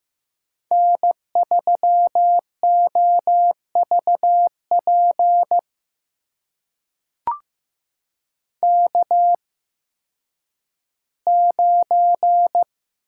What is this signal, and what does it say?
Morse code, heard as a keyed tone, about 15 wpm, spaced as for 10 wpm, 700 Hz. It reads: N3OVP K9
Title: USA Call Signs - 5 Characters - 8x Character Spacing - 15wpm